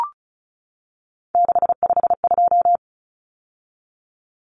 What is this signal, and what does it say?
Morse code, heard as a keyed tone, about 34 wpm, 700 Hz. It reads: 652
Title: Common Cut Numbers - 3-Digit 35wpm